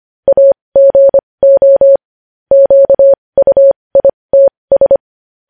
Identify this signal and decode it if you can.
Morse code, heard as a keyed tone, about 25 wpm, spaced as for 20 wpm, 550 Hz. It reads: AGO QUITS
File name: podcast / News headlines-1x-25x20